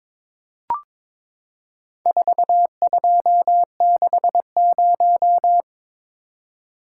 Morse code, heard as a keyed tone, about 22 wpm, 700 Hz. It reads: 4260
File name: practice / 22wpm / Four Letter-Number Combinations - 22wpm